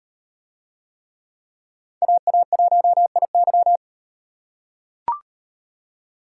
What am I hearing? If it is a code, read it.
Morse code, heard as a keyed tone, about 35 wpm, 700 Hz. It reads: AA1IY